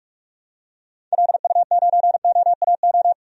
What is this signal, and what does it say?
Morse code, heard as a keyed tone, about 47 wpm, 700 Hz. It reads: LU9OAO
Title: DXCC-100 Call Signs - 6 Characters - 45wpm